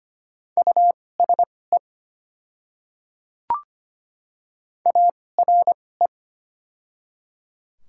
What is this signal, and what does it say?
Morse code, heard as a keyed tone, about 25 wpm, spaced as for 18 wpm, 700 Hz. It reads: USE ARE
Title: 2x Character Spacing - Top 100 Words - 25wpm